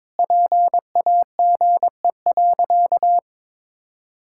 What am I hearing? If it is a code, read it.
Morse code, heard as a keyed tone, about 22 wpm, 700 Hz. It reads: PAGE.
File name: practice / 22wpm / Sentences with 4-Word Phrases - 22wpm